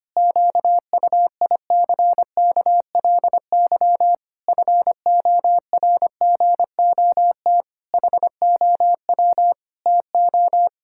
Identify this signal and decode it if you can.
Morse code, heard as a keyed tone, about 25 wpm, 700 Hz. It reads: QUICKLY FORGOT HOW TO
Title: Pangram Challenge v2 - 25wpm